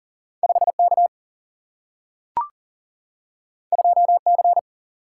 Morse code, heard as a keyed tone, about 39 wpm, 700 Hz. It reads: 5K 2C
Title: Single-Letter Single-Number 40wpm